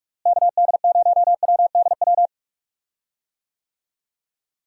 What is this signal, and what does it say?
Morse code, heard as a keyed tone, about 46 wpm, 700 Hz. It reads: KD0WDW
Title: USA Call Signs - 6 Characters - Repeated 3x - 45wpm